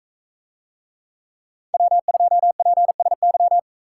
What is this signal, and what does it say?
Morse code, heard as a keyed tone, about 40 wpm, 700 Hz. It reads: W2PSY